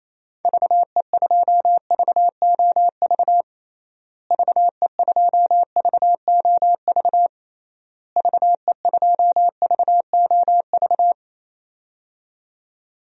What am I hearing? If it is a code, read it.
Morse code, heard as a keyed tone, about 28 wpm, 700 Hz. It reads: VE2VOV VE2VOV VE2VOV
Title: DXCC-100 Call Signs - 6 Characters - Repeated 3x - 28wpm